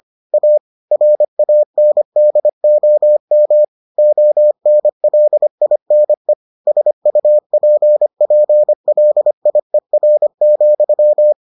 Morse code, heard as a keyed tone, about 25 wpm, 600 Hz. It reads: A RANDOM ONLINE SUPPLIER,